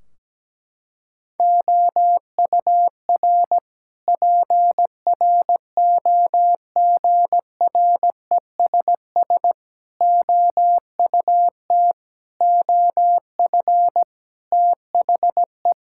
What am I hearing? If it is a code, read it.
Morse code, heard as a keyed tone, about 17 wpm, 700 Hz. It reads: OUR PROGRESS OUT OF THE
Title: Sentences from Top 1600 Words 17wpm